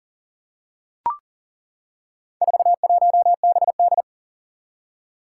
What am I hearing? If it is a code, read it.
Morse code, heard as a keyed tone, about 40 wpm, 700 Hz. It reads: 41BD
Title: Four Challenging Letter-Number Combinations 40wpm